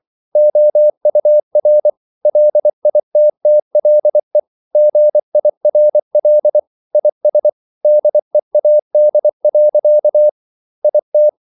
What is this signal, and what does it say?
Morse code, heard as a keyed tone, about 24 wpm, 600 Hz. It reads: OUR LITTLE GIRL IS DEAD. IT